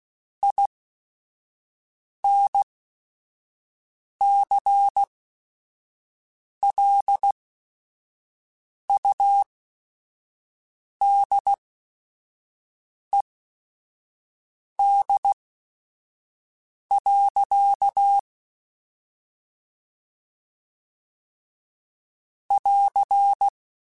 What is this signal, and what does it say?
Morse code, heard as a keyed tone, about 16 wpm, spaced as for 5 wpm, 800 Hz. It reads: INCLUDED. <AR>
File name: Chuckles / Chuckles-mp3s / Guaranteed @5